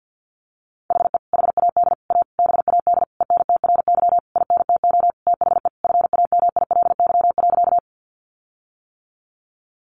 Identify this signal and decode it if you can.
Morse code, heard as a keyed tone, about 50 wpm, 700 Hz. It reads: HE HAD A BAD ENTRY INTO THE FAMILY.
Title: Sentences from Top 1500 Words 50wpm